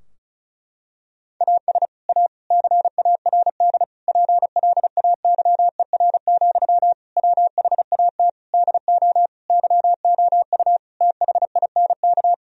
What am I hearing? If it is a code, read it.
Morse code, heard as a keyed tone, about 34 wpm, 700 Hz. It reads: AS A CARD PLAYER, WHAT DO YOU THINK
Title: Sentences from Top 1000 Words - Review Part 4 - 35wpm